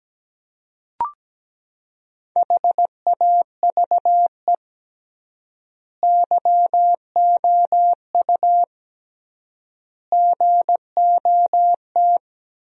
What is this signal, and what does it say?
Morse code, heard as a keyed tone, about 17 wpm, 700 Hz. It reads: HAVE YOU GOT
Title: 3x Word-Spacing - Sentences from Top 300 Words - 17wpm